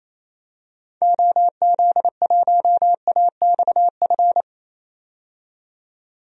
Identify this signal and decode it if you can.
Morse code, heard as a keyed tone, about 27 wpm, 700 Hz. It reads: OZ1AXF
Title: DXCC-100 Call Signs - 6 Characters - 28wpm